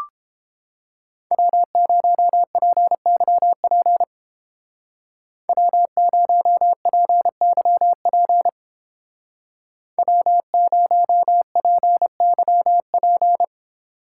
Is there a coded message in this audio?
Yes: Morse code from 1.3 to 13.4 s, heard as a keyed tone, about 31 wpm, 700 Hz. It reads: W0PYP W0PYP W0PYP